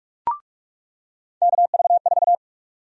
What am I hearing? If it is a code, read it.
Morse code, heard as a keyed tone, about 46 wpm, 700 Hz. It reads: KV4
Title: USA Call Sign Prefixes - 45wpm